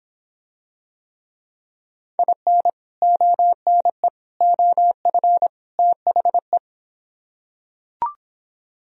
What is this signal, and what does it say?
Morse code, heard as a keyed tone, about 26 wpm, 700 Hz. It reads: IN ONE OF THE